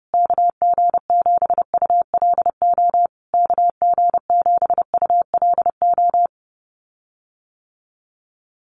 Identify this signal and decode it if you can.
Morse code, heard as a keyed tone, about 30 wpm, 700 Hz. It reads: KG7ULO KG7ULO
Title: USA Call Signs Repeated 3x 30wpm